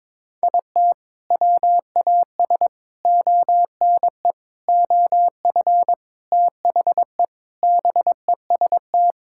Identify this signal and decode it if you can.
Morse code, heard as a keyed tone, about 21 wpm, 700 Hz. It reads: IT WAS ONE OF THE BEST
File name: practice / 22wpm / Sentences with 3-Word Phrases - 22wpm